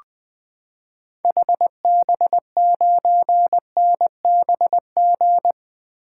Morse code, heard as a keyed tone, about 20 wpm, 700 Hz. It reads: HB9NBG
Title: 2023 SOTA Activator Call Signs - 20wpm